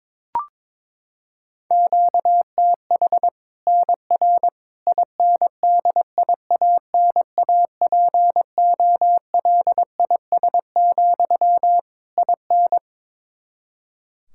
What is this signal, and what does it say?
Morse code, heard as a keyed tone, about 22 wpm, 700 Hz. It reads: QTH NR INDIANAPOLIS, IN
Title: Practice QSO - Exchange 1 - Beginning - 22wpm